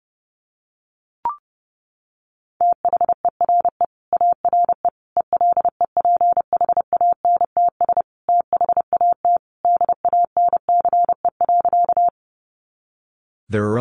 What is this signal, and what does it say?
Morse code, heard as a keyed tone, about 30 wpm, 700 Hz. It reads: THERE ARE ELEPHANTS THAT DANCE.
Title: Easy Phrases 30wpm